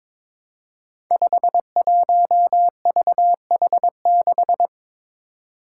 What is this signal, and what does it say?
Morse code, heard as a keyed tone, about 23 wpm, 700 Hz. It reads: 51VH6
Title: Five Challenging Letter-Number Combinations 22wpm